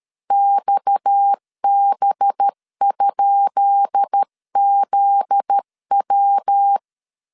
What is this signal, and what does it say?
Morse code, heard as a keyed tone, about 13 wpm, 800 Hz. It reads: XB?ZW
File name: LEZIONI / 10 SIGNS / 10B INT 12wpm 2414